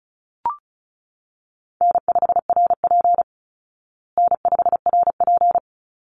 Morse code, heard as a keyed tone, about 34 wpm, 700 Hz. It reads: N5RP N5RP
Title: Easy Call Signs Repeated 2x - 35wpm